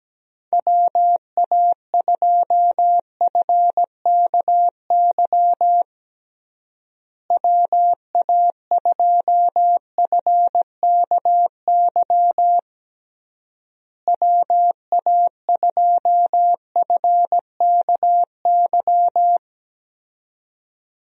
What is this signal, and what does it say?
Morse code, heard as a keyed tone, about 17 wpm, 700 Hz. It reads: WA2FKY WA2FKY WA2FKY
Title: USA Call Signs - 6 Characters - Repeated 3x - 17wpm